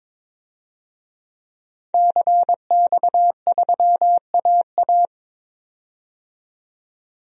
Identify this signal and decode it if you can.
Morse code, heard as a keyed tone, about 22 wpm, 700 Hz. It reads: CX3AA
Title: DXCC-100 Call Signs - 5 Characters - 22wpm